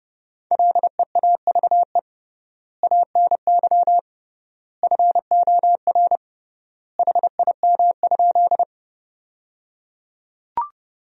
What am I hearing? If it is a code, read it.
Morse code, heard as a keyed tone, about 30 wpm, 700 Hz. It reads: LEAVE ANY FOR HIM?
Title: 3x Word-Spacing - Sentences from Top 400 Words - 30wpm